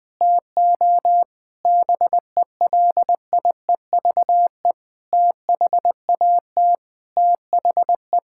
Morse code, heard as a keyed tone, about 20 wpm, 700 Hz. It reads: TO BELIEVE THAT THE
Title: Sentences with 7-Word Phrases - Spoken Beginning - 20wpm